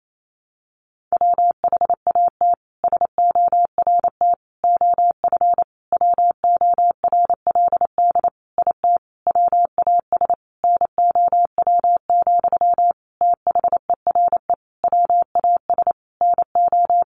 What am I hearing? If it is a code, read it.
Morse code, heard as a keyed tone, about 28 wpm, 700 Hz. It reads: WHAT SORT OF WORLD IT WAS NOW, THERE WAS NO
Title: A Sound Of Thunder 28wpm